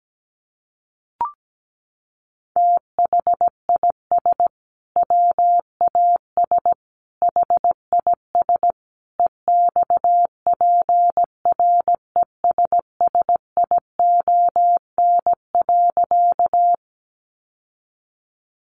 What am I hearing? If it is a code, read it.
Morse code, heard as a keyed tone, about 17 wpm, 700 Hz. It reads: THIS WAS HIS EXPRESSION.
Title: Sentences from Top 1200 Words 17wpm